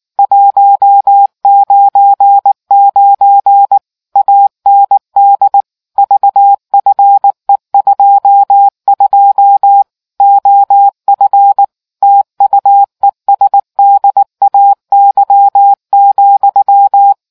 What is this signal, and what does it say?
Morse code, heard as a keyed tone, about 19 wpm, 800 Hz. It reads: 199 AND VFE22 OF TUESDAY,